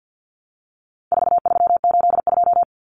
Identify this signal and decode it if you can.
Morse code, heard as a keyed tone, about 50 wpm, 700 Hz. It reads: 4382
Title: Common Cut Numbers - 4-Digit 50wpm